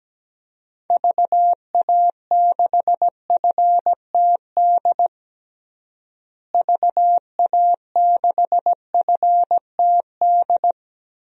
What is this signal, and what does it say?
Morse code, heard as a keyed tone, about 17 wpm, 700 Hz. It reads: VA6FTD VA6FTD
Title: DXCC-100 Call Signs - 6 Characters - Repeated 3x - 17wpm